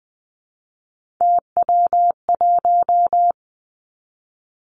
Morse code, heard as a keyed tone, about 20 wpm, 700 Hz. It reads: TW1
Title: Three Challenging Letter-Number Combinations 20wpm